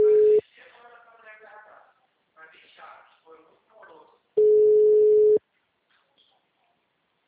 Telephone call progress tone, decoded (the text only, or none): ringback tone